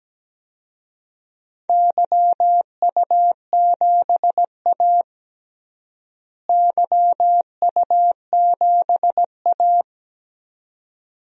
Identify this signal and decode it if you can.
Morse code, heard as a keyed tone, about 17 wpm, 700 Hz. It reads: YU7A YU7A
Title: DXCC-100 Call Signs - 4 Characters - Repeated 2x - 17wpm